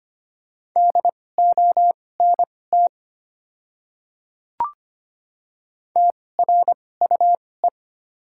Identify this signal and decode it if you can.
Morse code, heard as a keyed tone, about 25 wpm, spaced as for 18 wpm, 690 Hz. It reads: DONT TRUE
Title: 2x Character Spacing - Top 500 Words - 25wpm